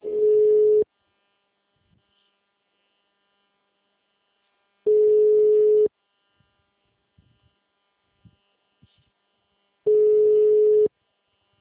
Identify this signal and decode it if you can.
Ringback tone (Europe, 425 Hz), with speech.
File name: out- -2016- - - .950